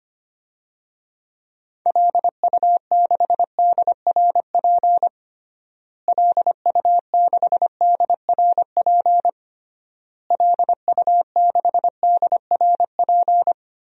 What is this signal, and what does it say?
Morse code, heard as a keyed tone, about 25 wpm, 700 Hz. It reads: LU6DRP LU6DRP LU6DRP
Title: DXCC-100 Call Signs - 6 Characters - Repeated 3x - 25wpm